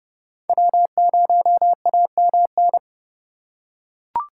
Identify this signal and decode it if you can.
Morse code, heard as a keyed tone, about 30 wpm, 700 Hz. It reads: W0AMN
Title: USA Call Signs - 5 Characters - 4x Character Spacing - 30wpm